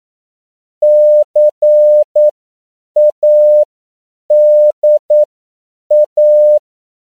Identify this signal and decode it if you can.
Morse code, heard as a keyed tone, about 9 wpm, spaced as for 7 wpm, 600 Hz. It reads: CADA